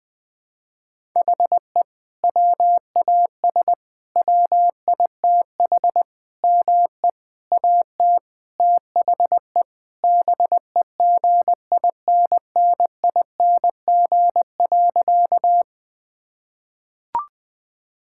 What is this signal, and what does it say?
Morse code, heard as a keyed tone, about 21 wpm, 700 Hz. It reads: HE WAS WITH ME AT THE BEGINNING.
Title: Sentences from Top 1000 Words - Review Part 5 - 20wpm